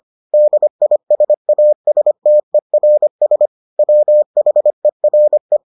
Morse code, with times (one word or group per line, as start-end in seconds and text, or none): 0.3-3.5 DISASTERS
3.8-5.6 WHERE